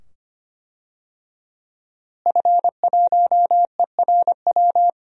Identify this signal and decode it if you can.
Morse code, heard as a keyed tone, about 25 wpm, 700 Hz. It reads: F1ERW
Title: DXCC-100 Call Signs - 5 Characters - Repeated 2x - 25wpm